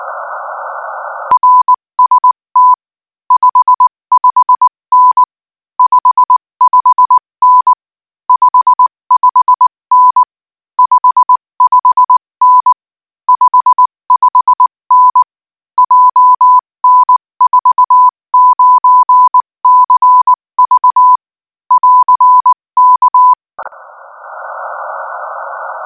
Morse code, heard as a keyed tone, about 19 wpm, 1000 Hz. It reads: RST 55N 55N 55N 55N 55N JN49CV <AR> K
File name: IZ4BEH-13-cm-AS